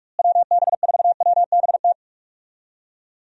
Morse code, heard as a keyed tone, about 44 wpm, 700 Hz. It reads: WB4WBT
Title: USA Call Signs - 6 Characters - Repeated 3x - 45wpm